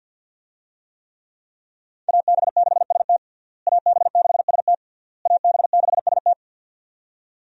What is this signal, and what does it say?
Morse code, heard as a keyed tone, about 50 wpm, 700 Hz. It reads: AB6ST AB6ST AB6ST
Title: USA Call Signs - 5 Characters - Repeated 3x - 50wpm